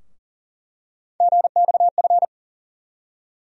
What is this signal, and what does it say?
Morse code, heard as a keyed tone, about 40 wpm, 700 Hz. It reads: GXF
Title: USA Call Sign Postfixes - 3 Letters 40wpm